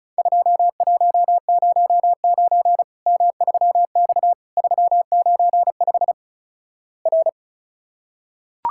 Morse code, heard as a keyed tone, about 36 wpm, 700 Hz. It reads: 2109 M3X 395 R
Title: Worked All Europe DX Contest - QTC Exchange - 35wpm